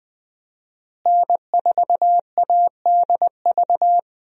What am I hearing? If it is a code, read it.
Morse code, heard as a keyed tone, about 20 wpm, 700 Hz. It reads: N4ADV
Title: USA Call Signs - 5 Characters - Repeated 3x - 20wpm